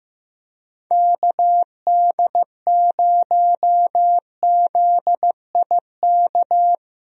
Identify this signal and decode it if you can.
Morse code, heard as a keyed tone, about 15 wpm, 700 Hz. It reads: KD0ZIK
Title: USA Call Signs - 6 Characters - Repeated 3x - 15wpm